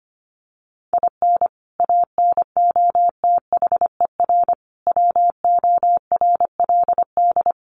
Morse code, heard as a keyed tone, about 25 wpm, 700 Hz. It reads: IN ANOTHER WORLD
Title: Sentences from Top 1100 Words 25wpm